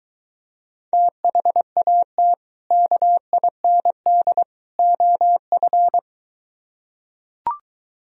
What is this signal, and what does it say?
Morse code, heard as a keyed tone, about 23 wpm, 700 Hz. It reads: THAT KIND OF